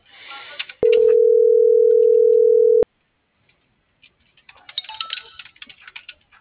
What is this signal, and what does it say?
Ringback tone (North America), 440 + 480 Hz.
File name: CENTRAL SAGOT